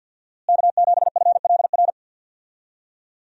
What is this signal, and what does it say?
Morse code, heard as a keyed tone, about 52 wpm, 700 Hz. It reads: K7FLR
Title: USA Call Signs - 5 Characters - Repeated 3x - 50wpm